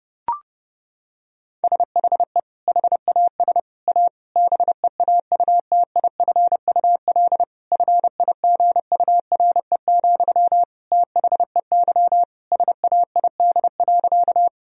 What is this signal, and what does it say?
Morse code, heard as a keyed tone, about 32 wpm, 700 Hz. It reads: SHE HAS A BEAUTIFUL FIGURE, THEY SAID.
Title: Sentences from Top 1000 Words - Review Part 2 - 30wpm